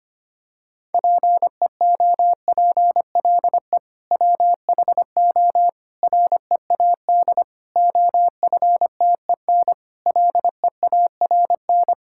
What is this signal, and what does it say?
Morse code, heard as a keyed tone, about 25 wpm, 700 Hz. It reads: PEOPLE WHO READ OFTEN LEARN
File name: practice / 25wpm / Sentences with 2-Word Phrases - 25wpm